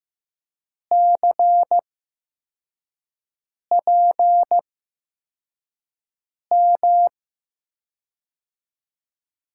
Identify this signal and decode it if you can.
Morse code, heard as a keyed tone, about 15 wpm, spaced as for 4 wpm, 700 Hz. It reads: CPM
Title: USA Call Signs - 6 Characters - 8x Character Spacing - 15wpm